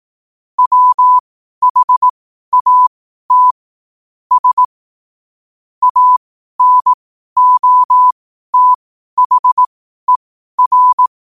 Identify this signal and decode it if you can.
Morse code, heard as a keyed tone, about 18 wpm, spaced as for 12 wpm, 1000 Hz. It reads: WHAT S ANOTHER
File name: qotdfiles / 12wpm